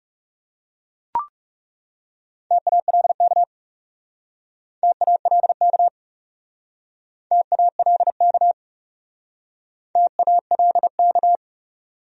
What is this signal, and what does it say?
Morse code, heard as a keyed tone, about 36 wpm, 700 Hz. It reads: TALK TALK TALK TALK
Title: Speed-Racing Top 500 Words 30wpm